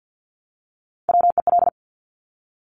Morse code, heard as a keyed tone, about 50 wpm, 700 Hz. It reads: WEL
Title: Top Three Letter Combinations in Words 50wpm